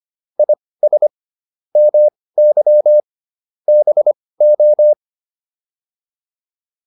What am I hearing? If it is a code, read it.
Morse code, heard as a keyed tone, about 25 wpm, spaced as for 18 wpm, 600 Hz. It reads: IS MY BO